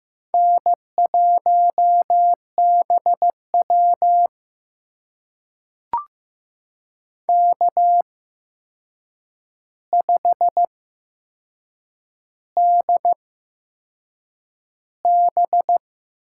Morse code, heard as a keyed tone, about 15 wpm, spaced as for 6 wpm, 700 Hz. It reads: N1BW K5DB
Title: USA Call Signs - 4 Characters - 8x Character Spacing - 15wpm